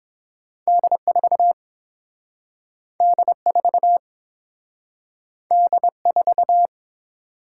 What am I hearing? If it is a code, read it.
Morse code, heard as a keyed tone, about 30 wpm, 700 Hz. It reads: D 4 D 4 D 4